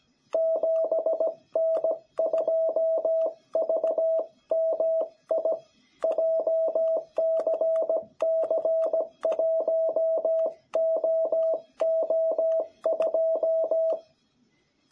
Morse code, heard as a keyed tone, about 17 wpm, 660 Hz. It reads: 7N2VMI JCC1OO2
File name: jh8cbh / 2021cw